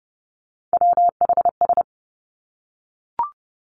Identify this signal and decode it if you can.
Morse code, heard as a keyed tone, about 30 wpm, 700 Hz. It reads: WHS
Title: Three Challenging Letter-Number Combinations 30wpm